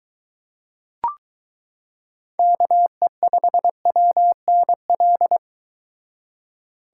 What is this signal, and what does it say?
Morse code, heard as a keyed tone, about 23 wpm, 700 Hz. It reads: KE5WNL